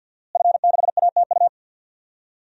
Morse code, heard as a keyed tone, about 46 wpm, 700 Hz. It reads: F6ATU
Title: DXCC-100 Call Signs - 5 Characters - Repeated 3x - 50wpm